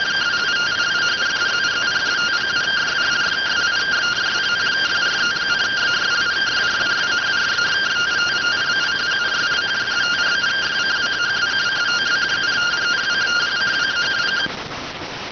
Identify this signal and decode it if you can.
Two-tone teleprinter signal, about 170 Hz shift.